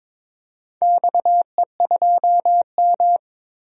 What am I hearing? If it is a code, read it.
Morse code, heard as a keyed tone, about 22 wpm, 700 Hz. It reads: XE2M
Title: DXCC-100 Call Signs - 4 Characters - 22wpm